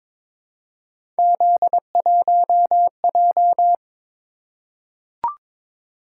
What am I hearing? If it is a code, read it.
Morse code, heard as a keyed tone, about 22 wpm, 700 Hz. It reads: Z1J
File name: practice / 22wpm / Three Letter-Number Combinations - 22wpm